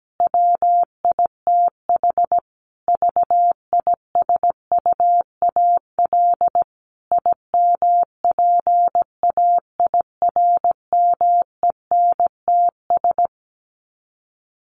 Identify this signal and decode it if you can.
Morse code, heard as a keyed tone, about 17 wpm, 700 Hz. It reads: WITH VISUAL IMPAIRMENTS
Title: Sets of 5 Words Encore 17wpm